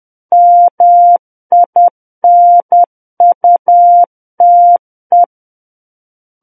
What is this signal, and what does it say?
Morse code, heard as a keyed tone, about 10 wpm, 700 Hz. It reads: MINUTE
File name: Book One / 01-Chapter 1 - 10 WPM